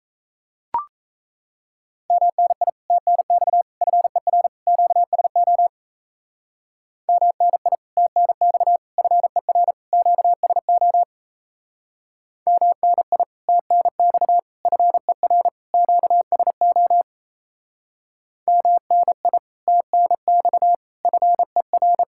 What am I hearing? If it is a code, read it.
Morse code, heard as a keyed tone, about 35 wpm, 700 Hz. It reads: MNI TNX FER QSO MNI TNX FER QSO MNI TNX FER QSO MNI TNX FER
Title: Speed-Racing Common QSO Phrases - 28wpm